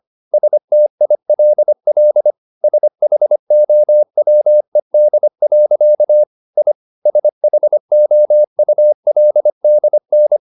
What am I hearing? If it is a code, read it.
Morse code, heard as a keyed tone, about 25 wpm, 600 Hz. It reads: STILL SHOWED. I SHOULDN